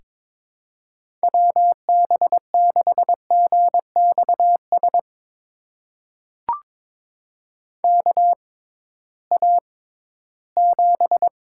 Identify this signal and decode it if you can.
Morse code, heard as a keyed tone, about 22 wpm, spaced as for 14 wpm, 700 Hz. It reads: WB6GXS KA7
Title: USA Call Signs - 6 Characters - 6x Character Spacing - 22wpm